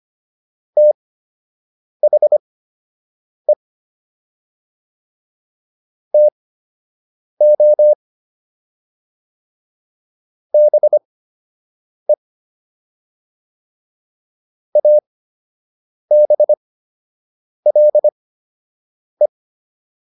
Morse code, heard as a keyed tone, about 25 wpm, spaced as for 7 wpm, 600 Hz. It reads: THE TO BE ABLE